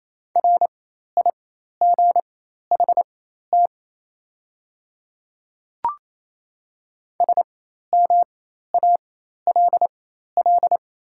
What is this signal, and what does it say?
Morse code, heard as a keyed tone, about 28 wpm, spaced as for 13 wpm, 700 Hz. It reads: RIGHT SMALL
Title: 4x Character Spacing - Top 200 Words - 28wpm